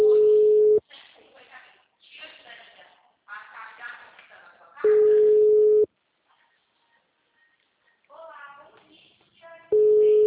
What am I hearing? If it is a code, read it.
Ringback tone (Europe, 425 Hz), with speech.